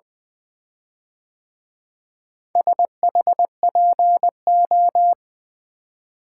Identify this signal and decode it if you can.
Morse code, heard as a keyed tone, about 24 wpm, 700 Hz. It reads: SHPO
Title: Speed-Racing - Four Letter-Number Combinations - 20wpm